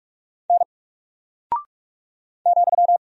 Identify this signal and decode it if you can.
Morse code, heard as a keyed tone, about 44 wpm, 700 Hz. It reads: N ,
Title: Single Character (Letter:Number:Punctuation:Prosign) Alt 45wpm